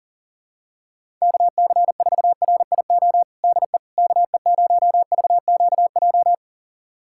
Morse code, heard as a keyed tone, about 40 wpm, 700 Hz. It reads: KC4RIO DE KE0VQJ
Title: USA Callsign DE Callsign 40wpm